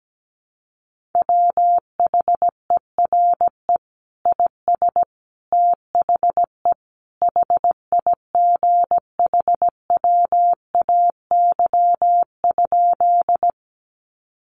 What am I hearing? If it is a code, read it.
Morse code, heard as a keyed tone, about 17 wpm, 700 Hz. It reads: WHERE IS THE HIGHWAY?
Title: Sentences from Top 1400 Words 17wpm